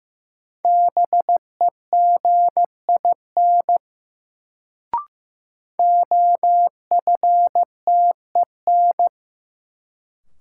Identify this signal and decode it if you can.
Morse code, heard as a keyed tone, about 15 wpm, 700 Hz. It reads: BEGIN OFTEN
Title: Sentences from Top 300 Words - New Word Drill - 15wpm